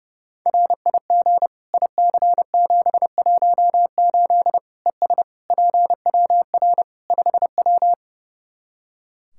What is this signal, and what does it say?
Morse code, heard as a keyed tone, about 32 wpm, 700 Hz. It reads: RIG IC718 ES PWR 5W
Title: Practice QSO - Exchange 2 - Info - 30wpm